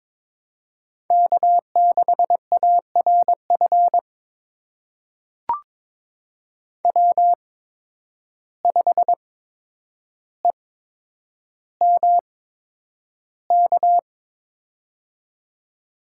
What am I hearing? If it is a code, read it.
Morse code, heard as a keyed tone, about 21 wpm, spaced as for 9 wpm, 700 Hz. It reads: K6ARF W5EMK
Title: USA Call Signs - 5 Characters - 8x Character Spacing - 22wpm